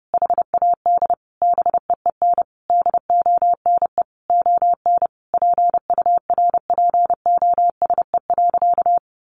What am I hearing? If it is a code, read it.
Morse code, heard as a keyed tone, about 30 wpm, 700 Hz. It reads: HAD BEEN DONE ON PURPOSE.
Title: Sentences from Top 800 Words 30wpm